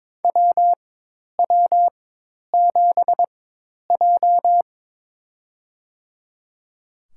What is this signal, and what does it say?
Morse code, heard as a keyed tone, about 22 wpm, spaced as for 10 wpm, 700 Hz. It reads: WW7J
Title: USA Call Signs - 4x Character Spacing - 22wpm